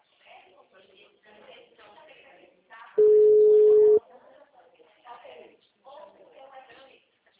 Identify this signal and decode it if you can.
Ringback tone (Europe, 425 Hz), with speech.
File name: out- -2011- - - .6486